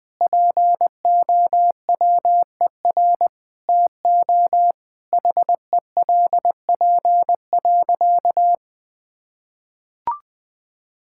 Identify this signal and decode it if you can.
Morse code, heard as a keyed tone, about 21 wpm, 700 Hz. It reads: POWER TO HELP.
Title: Sentences with 8-Word Phrases - Spoken Beginning - 20wpm